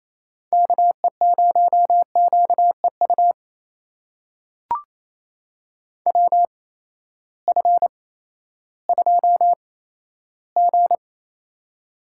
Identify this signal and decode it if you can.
Morse code, heard as a keyed tone, about 28 wpm, spaced as for 14 wpm, 700 Hz. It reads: KE0QEU WF2G
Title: USA Call Signs - 8x Character Spacing - 28wpm